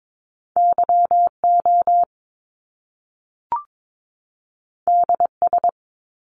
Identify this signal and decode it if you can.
Morse code, heard as a keyed tone, about 22 wpm, 700 Hz. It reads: YO DS
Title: Top Two Letter Combinations in Words 22wpm